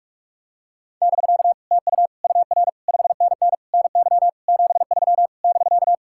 Morse code, heard as a keyed tone, about 40 wpm, 700 Hz. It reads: <BK> TU UR 5NN NY 73 <BK>